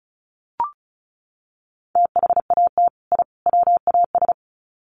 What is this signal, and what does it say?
Morse code, heard as a keyed tone, about 34 wpm, 700 Hz. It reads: THAT I WAS
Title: Sets of 3 Words 35wpm